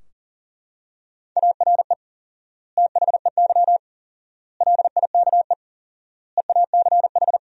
Morse code, heard as a keyed tone, about 40 wpm, 700 Hz. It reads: ARE THEY LIKE EACH
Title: 4x Word-Spacing - Sentences from Top 200 Words - 40wpm